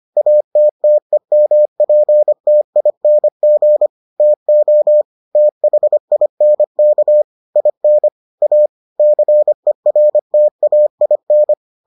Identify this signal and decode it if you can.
Morse code, heard as a keyed tone, about 25 wpm, 600 Hz. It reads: ATTEMPTING TO THINK IN A CERTAIN